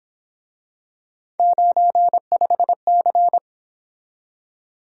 Morse code, heard as a keyed tone, about 26 wpm, 700 Hz. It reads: 95C
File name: Speed-Racing - Three Letter-Number Combinations - 22wpm